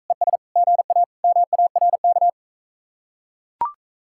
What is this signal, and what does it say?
Morse code, heard as a keyed tone, about 43 wpm, 700 Hz. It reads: ES GA MARK